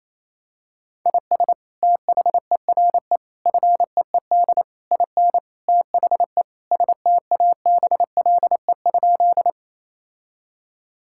Morse code, heard as a keyed tone, about 28 wpm, 700 Hz. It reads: IS THERE FEED IN THE STABLE?
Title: Sentences from Top 1600 Words 28wpm